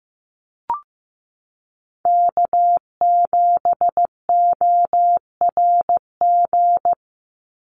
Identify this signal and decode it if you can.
Morse code, heard as a keyed tone, about 15 wpm, 700 Hz. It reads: K7ORG
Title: NA SOTA Chaser Callsigns 15wpm